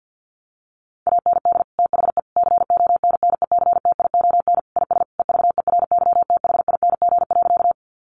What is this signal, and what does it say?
Morse code, heard as a keyed tone, about 48 wpm, 700 Hz. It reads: AND THE CONNECTION IS EVERYTHING.
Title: Sentences from Top 1300 Words 50wpm